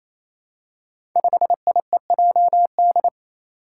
Morse code, heard as a keyed tone, about 28 wpm, 700 Hz. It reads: 5IEJD
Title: Five Challenging Letter-Number Combinations 28wpm